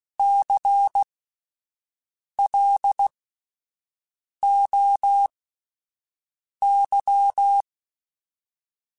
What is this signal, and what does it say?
Morse code, heard as a keyed tone, about 16 wpm, spaced as for 5 wpm, 800 Hz. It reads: CLOY
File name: Spaced Alphas / Spaced Alphas mp3s / Cs sp @10